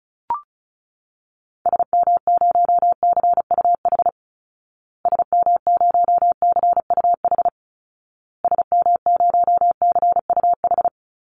Morse code, heard as a keyed tone, about 34 wpm, 700 Hz. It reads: SM0CUH SM0CUH SM0CUH
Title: Call Signs Repeated 2x - 35wpm